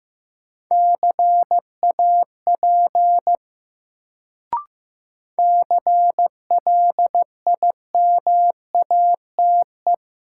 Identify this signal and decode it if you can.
Morse code, heard as a keyed tone, about 15 wpm, 700 Hz. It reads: CAP CLIMATE
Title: Sentences from Top 1600 Words - New Word Drill Alt - 15wpm